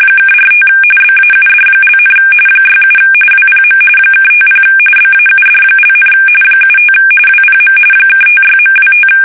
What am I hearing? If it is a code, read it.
RTTY, about 45.45 baud. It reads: (45 BAUD), 67 WPM (50 BAUD), 75 WPM (56.25
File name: fsk850 / teletype-RTTY